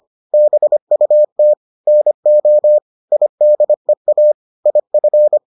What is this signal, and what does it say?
Morse code, heard as a keyed tone, about 25 wpm, 600 Hz. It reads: BUT NO IDEA IF